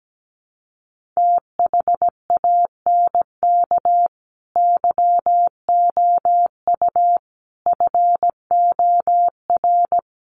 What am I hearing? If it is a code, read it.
Morse code, heard as a keyed tone, about 17 wpm, 700 Hz. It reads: THANK YOU FOR
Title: Sets of 5 Words 17wpm